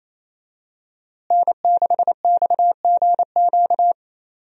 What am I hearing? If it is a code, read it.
Morse code, heard as a keyed tone, about 28 wpm, 700 Hz. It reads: N6XGQ
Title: USA Call Signs - 5 Characters - Repeated 3x - 28wpm